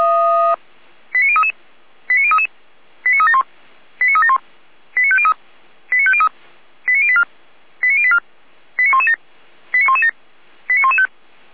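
Five-tone selective call (ZVEI), heard as quick runs of five tones, tones of about 70 ms. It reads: 89033 89033 89371 89371 89503 89503 89084 89084 89117 89117 89115
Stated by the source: Five-tone ZVEI selective call received by SDR.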